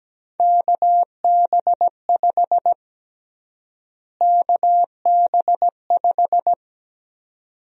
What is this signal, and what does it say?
Morse code, heard as a keyed tone, about 17 wpm, 700 Hz. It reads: KB5 KB5
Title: USA Call Sign Prefixes - Repeated 2x - 17wpm